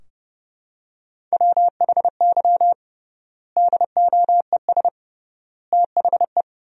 Morse code, heard as a keyed tone, about 30 wpm, 700 Hz. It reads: WHY DOES THE
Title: 3x Word-Spacing - Sentences from Top 500 Words - 30wpm